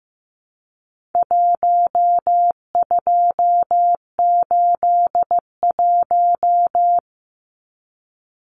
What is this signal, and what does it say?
Morse code, heard as a keyed tone, about 15 wpm, 700 Hz. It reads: 1281
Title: Numbers - 4-Digit 15wpm